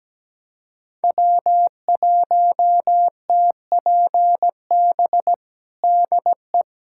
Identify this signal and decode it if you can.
Morse code, heard as a keyed tone, about 17 wpm, 700 Hz. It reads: W1TPB DE
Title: USA Callsign DE Callsign - 5 Characters - 17wpm